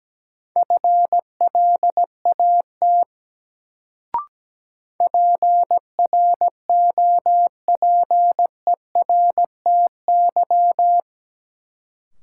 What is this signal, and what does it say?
Morse code, heard as a keyed tone, about 17 wpm, 700 Hz. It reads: FLAT PROPERTY
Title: Sentences from Top 1000 Words - New Word Drill - 17wpm